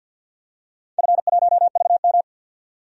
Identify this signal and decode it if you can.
Morse code, heard as a keyed tone, about 53 wpm, 700 Hz. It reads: F1VM
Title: DXCC-100 Call Signs - 4 Characters - 50wpm